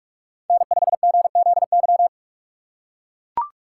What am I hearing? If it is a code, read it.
Morse code, heard as a keyed tone, about 38 wpm, 700 Hz. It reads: N5GZY